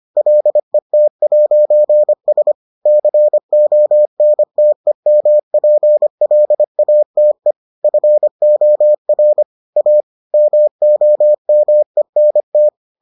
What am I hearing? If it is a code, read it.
Morse code, heard as a keyed tone, about 25 wpm, 600 Hz. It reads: LET'S CONTEMPLATE FOR A MOMENT